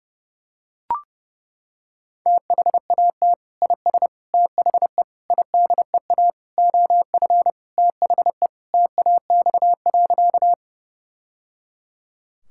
Morse code, heard as a keyed tone, about 30 wpm, 700 Hz. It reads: THAT IS THE IDEA OF THE TAX.
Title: Sentences from Top 1000 Words - Review Part 5 - 30wpm